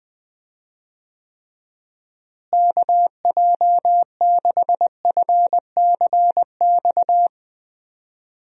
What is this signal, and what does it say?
Morse code, heard as a keyed tone, about 20 wpm, 700 Hz. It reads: KJ6FCX
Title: USA Call Signs - 6 Characters - Repeated 3x - 20wpm